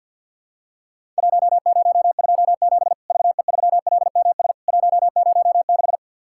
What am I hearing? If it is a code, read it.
Morse code, heard as a keyed tone, about 48 wpm, 700 Hz. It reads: 1027 VE3LMS 106
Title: Worked All Europe DX Contest - QTC Exchange - 50wpm